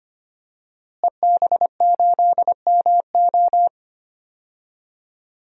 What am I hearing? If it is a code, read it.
Morse code, heard as a keyed tone, about 25 wpm, 700 Hz. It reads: EB8MO
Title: DXCC-100 Call Signs - 5 Characters - Repeated 2x - 25wpm